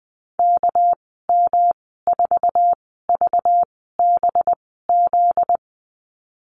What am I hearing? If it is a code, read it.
Morse code, heard as a keyed tone, about 19 wpm, spaced as for 14 wpm, 700 Hz. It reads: KM4VBZ
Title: 245 Lesson 39g - Letter z - Call Signs 20wpm